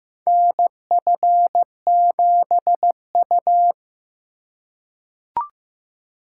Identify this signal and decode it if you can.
Morse code, heard as a keyed tone, about 15 wpm, 700 Hz. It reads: NF7U
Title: USA Call Signs - 4x Character Spacing - 15wpm